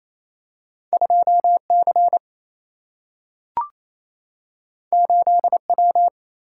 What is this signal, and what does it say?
Morse code, heard as a keyed tone, about 28 wpm, 700 Hz. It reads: 2 C 8 W
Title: Single-Letter Single-Number 28wpm